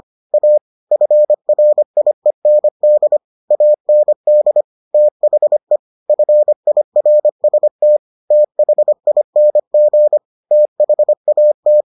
Morse code, heard as a keyed tone, about 25 wpm, 600 Hz. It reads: A FRIEND AND THE FIRST THING THAT